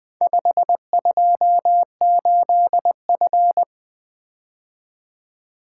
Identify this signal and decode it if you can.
Morse code, heard as a keyed tone, about 20 wpm, 700 Hz. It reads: 528F
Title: Four Letter-Number Combinations - 20wpm